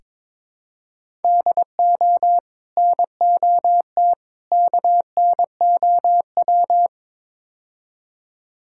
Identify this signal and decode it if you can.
Morse code, heard as a keyed tone, about 22 wpm, 700 Hz. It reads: DO NOT KNOW
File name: practice / 22wpm / Build A Phrase - 5-Word Phrases - 22wpm